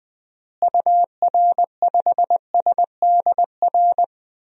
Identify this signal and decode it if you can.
Morse code, heard as a keyed tone, about 21 wpm, 700 Hz. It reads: UR5SDR
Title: DXCC-100 Call Signs - 6 Characters - Repeated 3x - 20wpm